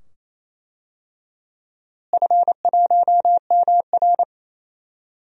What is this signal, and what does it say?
Morse code, heard as a keyed tone, about 28 wpm, 700 Hz. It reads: F1MR
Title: DXCC-100 Call Signs - 4 Characters - Repeated 2x - 28wpm